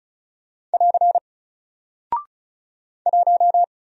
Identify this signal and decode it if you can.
Morse code, heard as a keyed tone, about 36 wpm, 700 Hz. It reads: <AR> 1
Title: Single Character (Letter:Number:Punctuation:Prosign) 35wpm